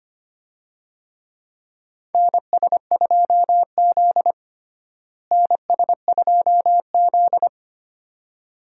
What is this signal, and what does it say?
Morse code, heard as a keyed tone, about 25 wpm, 700 Hz. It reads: NS2Z NS2Z
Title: USA Call Signs - 4 Characters - Repeated 2x - 25wpm